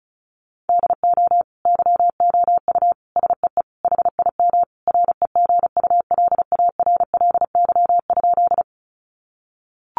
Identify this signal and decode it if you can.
Morse code, heard as a keyed tone, about 34 wpm, 700 Hz. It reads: DO YOU SEE HIM REGULARLY?
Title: Sentences from Top 1400 Words 35wpm